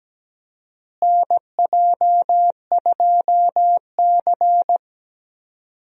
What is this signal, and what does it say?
Morse code, heard as a keyed tone, about 17 wpm, 700 Hz. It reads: NJ2C
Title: USA Call Signs - 4 Characters - Repeated 3x - 17wpm